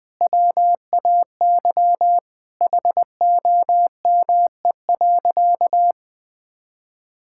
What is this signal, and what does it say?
Morse code, heard as a keyed tone, about 20 wpm, 700 Hz. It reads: WAY HOME.
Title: Sentences with 3-Word Phrases - 20wpm